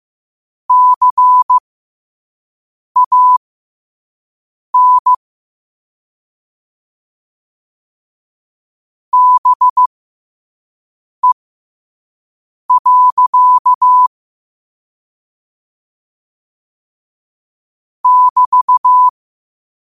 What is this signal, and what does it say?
Morse code, heard as a keyed tone, about 15 wpm, spaced as for 5 wpm, 1000 Hz. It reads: CAN BE. <BT>
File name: qotdfiles / 5wpm